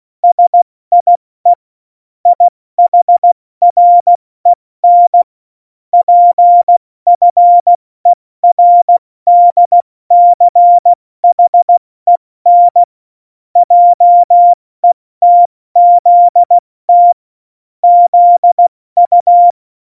Morse code, heard as a keyed tone, about 16 wpm, 700 Hz. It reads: SIE IHREN PFERDCHEN JETZT ZU
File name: Morsekurs / Witze / Witz 04 August 2020, 70 F80